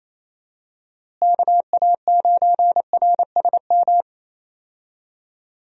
Morse code, heard as a keyed tone, about 27 wpm, 700 Hz. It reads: KA9RSM
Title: USA Call Signs - 6 Characters - Repeated 3x - 28wpm